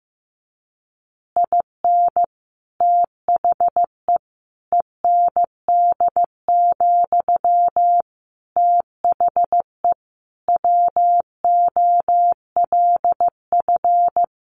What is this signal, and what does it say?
Morse code, heard as a keyed tone, about 15 wpm, 700 Hz. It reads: IN THE END, THE WOLF
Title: Additional Easy Phrases 15wpm